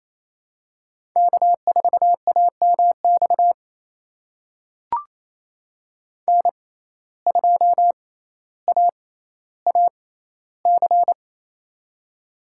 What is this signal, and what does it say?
Morse code, heard as a keyed tone, about 28 wpm, spaced as for 14 wpm, 700 Hz. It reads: K4AMX N2AAC
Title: USA Call Signs - 5 Characters - 6x Character Spacing - 28wpm